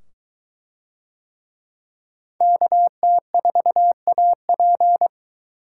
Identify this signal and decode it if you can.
Morse code, heard as a keyed tone, about 23 wpm, 700 Hz. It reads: KT4AP